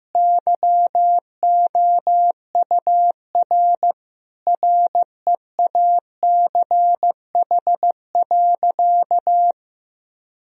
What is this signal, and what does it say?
Morse code, heard as a keyed tone, about 15 wpm, 700 Hz. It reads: YOUR REACH.
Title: Sentences from Top 1000 Words - Review Part 2 - 15wpm